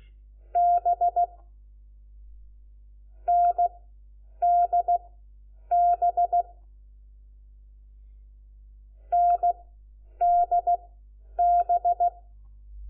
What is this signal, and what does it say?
Morse code, heard as a keyed tone, about 16 wpm, spaced as for 9 wpm, 700 Hz. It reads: B NDB NDB
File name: Lecciones / CLASE Nº4